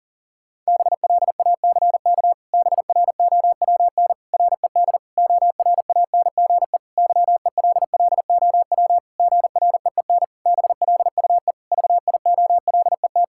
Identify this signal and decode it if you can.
Morse code, heard as a keyed tone, about 40 wpm, 700 Hz. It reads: BLACK BROWN RED ORANGE YELLOW GREEN BLUE VIOLET
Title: Pangram Challenge v2 - 40wpm